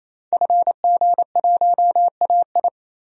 Morse code, heard as a keyed tone, about 28 wpm, 700 Hz. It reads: FG1AI
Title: DXCC-100 Call Signs - 5 Characters - 28wpm